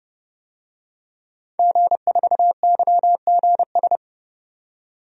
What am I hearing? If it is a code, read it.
Morse code, heard as a keyed tone, about 32 wpm, 700 Hz. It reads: G4YGS
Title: DXCC-100 Call Signs - 5 Characters - 30wpm